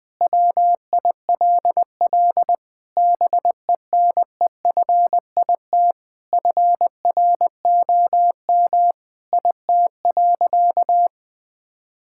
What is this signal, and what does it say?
Morse code, heard as a keyed tone, about 19 wpm, 700 Hz. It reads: WILL BENEFIT FROM IT.
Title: Sentences from Top 1000 Words - Review Part 6 - 20wpm